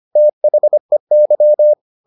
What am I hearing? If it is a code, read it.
Morse code, heard as a keyed tone, about 25 wpm, 600 Hz. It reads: THEY
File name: the old man and the sea part 1 25wpm